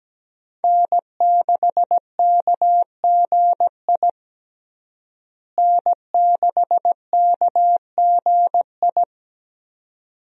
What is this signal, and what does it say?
Morse code, heard as a keyed tone, about 17 wpm, 700 Hz. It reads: N6KGI N6KGI
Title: USA Call Signs - 5 Characters - Repeated 2x - 17wpm